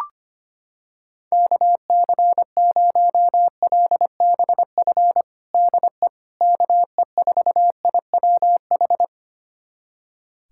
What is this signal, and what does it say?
Morse code, heard as a keyed tone, about 25 wpm, 700 Hz. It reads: KC0LBF DE KE4IWH
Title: USA Callsign DE Callsign - 6 Characters - 25wpm